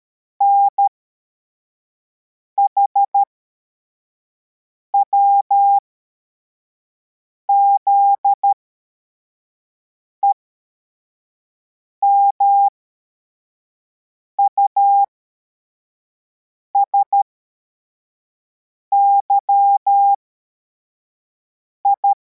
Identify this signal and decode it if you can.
Morse code, heard as a keyed tone, about 13 wpm, spaced as for 4 wpm, 800 Hz. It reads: NHWZE MUSYI